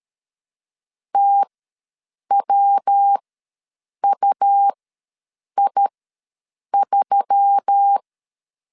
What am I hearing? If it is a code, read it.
Morse code, heard as a keyed tone, about 13 wpm, spaced as for 7 wpm, 800 Hz. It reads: TWUI3
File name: LEZIONI / 5 LBVC / 5 LB ws 12wpm 3034